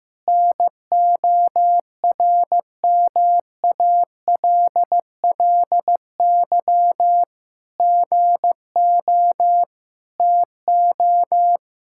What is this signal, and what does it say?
Morse code, heard as a keyed tone, about 15 wpm, 700 Hz. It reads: NORMALLY GO TO
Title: Sentences from Top 1000 Words - Review Part 6 - 15wpm